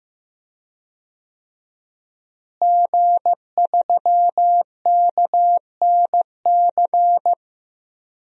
Morse code, heard as a keyed tone, about 15 wpm, 700 Hz. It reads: G3KNC
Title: DXCC-100 Call Signs - 5 Characters - Repeated 3x - 15wpm